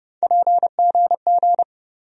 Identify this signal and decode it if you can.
Morse code, heard as a keyed tone, about 30 wpm, 700 Hz. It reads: PGG
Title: USA Call Sign Postfixes - 3 Letters 30wpm